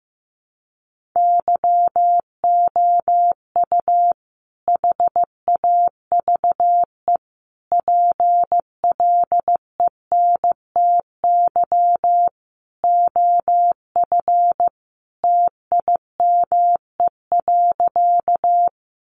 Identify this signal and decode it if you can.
Morse code, heard as a keyed tone, about 15 wpm, 700 Hz. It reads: YOU HAVE PLENTY OF TIME.
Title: Sentences from Top 1300 Words 15wpm